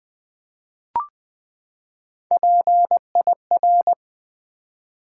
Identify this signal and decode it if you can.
Morse code, heard as a keyed tone, about 20 wpm, 700 Hz. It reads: PIR
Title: Build A Word - 3 Letters At A Time - 6-Letter Words - 20wpm